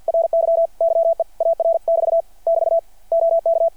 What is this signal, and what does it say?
Morse code, heard as a keyed tone, about 53 wpm, 660 Hz. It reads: RY YEAR<BT> <BT> OK
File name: archived / CRSnet